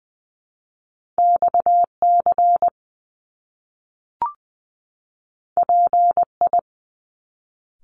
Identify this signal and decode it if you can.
Morse code, heard as a keyed tone, about 20 wpm, 700 Hz. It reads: XC PI
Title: Top Two Letter Combinations in Words 20wpm